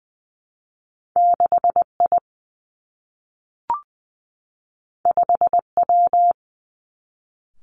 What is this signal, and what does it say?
Morse code, heard as a keyed tone, about 19 wpm, 700 Hz. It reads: 6 I 5 W
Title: Two Challenging Letter-Number Combinations 20wpm